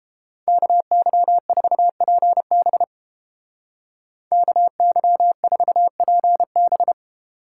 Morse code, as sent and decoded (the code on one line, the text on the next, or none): -.- -.-- ....- .--. -... / -.- -.-- ....- .--. -...
KY4PB KY4PB